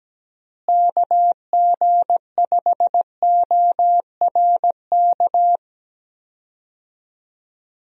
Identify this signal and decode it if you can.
Morse code, heard as a keyed tone, about 17 wpm, 700 Hz. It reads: KG5ORK
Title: USA Call Signs - 6 Characters - Repeated 2x - 17wpm